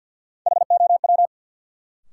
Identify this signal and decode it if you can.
Morse code, heard as a keyed tone, about 53 wpm, 700 Hz. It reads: HOW
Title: Sets of 2 Words - New Word Drill - 50wpm